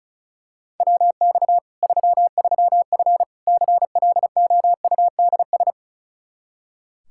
Morse code, heard as a keyed tone, about 34 wpm, 700 Hz. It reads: WX 33F CLOUDS
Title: Practice QSO - Exchange 2 - Info - 35wpm